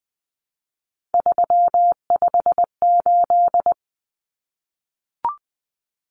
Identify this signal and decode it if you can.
Morse code, heard as a keyed tone, about 20 wpm, 700 Hz. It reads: 358
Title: Numbers - 3-Digit 20wpm